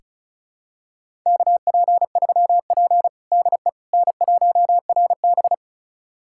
Morse code, heard as a keyed tone, about 36 wpm, 700 Hz. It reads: KP3P DE N1RB
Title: USA Callsign DE Callsign - 4 Characters - 35wpm